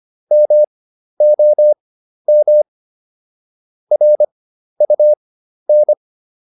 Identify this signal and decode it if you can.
Morse code, heard as a keyed tone, about 25 wpm, spaced as for 12 wpm, 600 Hz. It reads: MOM RUN